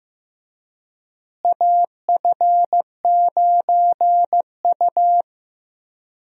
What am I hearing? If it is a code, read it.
Morse code, heard as a keyed tone, about 15 wpm, 700 Hz. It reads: AF9U
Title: USA Call Signs - 4 Characters - Repeated 2x - 15wpm